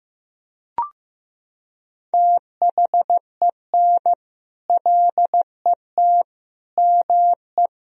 Morse code, heard as a keyed tone, about 16 wpm, 700 Hz. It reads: THEN LET ME
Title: Sentences from Top 1000 Words - Review Part 2 - 15wpm